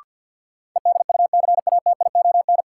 Morse code, heard as a keyed tone, about 52 wpm, 700 Hz. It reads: EDUCATION
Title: Sentences from Top 1100 Words - New Word Drill Alt - 50wpm